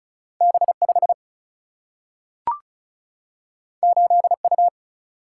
Morse code, heard as a keyed tone, about 36 wpm, 700 Hz. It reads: B5 8U
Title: Single-Letter Single-Number 35wpm